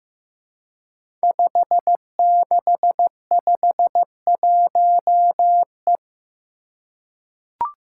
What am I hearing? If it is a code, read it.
Morse code, heard as a keyed tone, about 15 wpm, 700 Hz. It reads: 5651E
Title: Five Challenging Letter-Number Combinations 15wpm